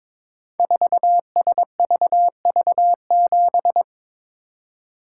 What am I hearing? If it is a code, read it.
Morse code, heard as a keyed tone, about 22 wpm, 700 Hz. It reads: 4SVV7
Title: Code Groups - 5 Letter:Numbers - 22wpm